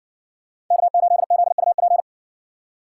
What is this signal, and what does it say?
Morse code, heard as a keyed tone, about 60 wpm, 700 Hz. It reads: K8ZUP